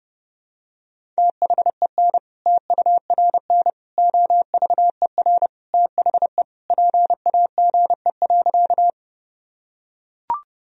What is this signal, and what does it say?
Morse code, heard as a keyed tone, about 30 wpm, 700 Hz. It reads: THEN TURN OVER THE PAGE.
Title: Sentences from Top 1000 Words - Review Part 1 - 30wpm